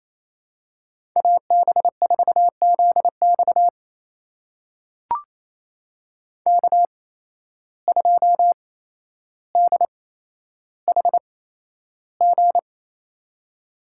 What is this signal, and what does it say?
Morse code, heard as a keyed tone, about 28 wpm, spaced as for 12 wpm, 700 Hz. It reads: AB4ZX K2DHG
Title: USA Call Signs - 5 Characters - 8x Character Spacing - 28wpm